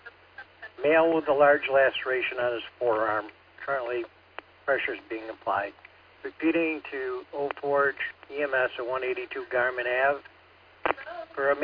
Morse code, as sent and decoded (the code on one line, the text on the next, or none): none